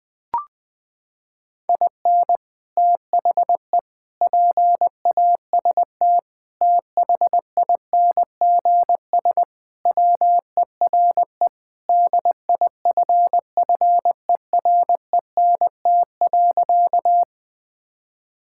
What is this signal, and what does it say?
Morse code, heard as a keyed tone, about 21 wpm, 700 Hz. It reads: IN THE PAST THINGS WERE DIFFERENT.
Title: Sentences with 3-Word Phrases - 20wpm